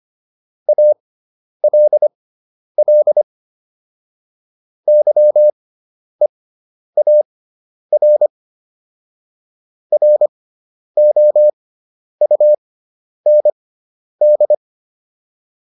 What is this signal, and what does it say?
Morse code, heard as a keyed tone, about 25 wpm, spaced as for 10 wpm, 600 Hz. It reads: ALL YEAR ROUND